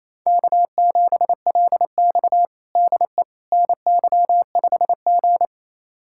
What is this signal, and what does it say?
Morse code, heard as a keyed tone, about 28 wpm, 700 Hz. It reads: K7LX DE NY5G
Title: USA Callsign DE Callsign - 4 Characters - 28wpm